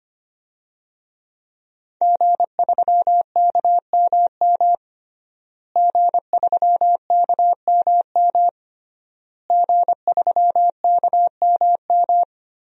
Morse code, heard as a keyed tone, about 25 wpm, 700 Hz. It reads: G3KMM G3KMM G3KMM
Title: DXCC-100 Call Signs - 5 Characters - Repeated 3x - 25wpm